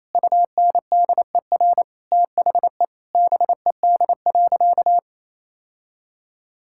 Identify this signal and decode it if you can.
Morse code, heard as a keyed tone, about 27 wpm, 700 Hz. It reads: UNDER THE BED.
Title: Sentences from Top 1000 Words - Review Part 5 - 28wpm